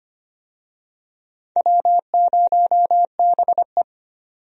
Morse code, heard as a keyed tone, about 25 wpm, 700 Hz. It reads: W0BE
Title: USA Call Signs - 4 Characters - 25wpm